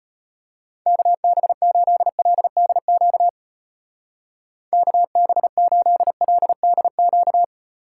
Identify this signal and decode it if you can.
Morse code, heard as a keyed tone, about 32 wpm, 700 Hz. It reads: KB8LDQ KB8LDQ